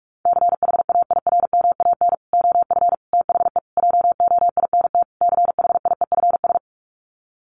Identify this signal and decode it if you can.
Morse code, heard as a keyed tone, about 44 wpm, 700 Hz. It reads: CHAIRMAN OF THE JOINT CHIEFS
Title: Sets of 5 Words Encore 45wpm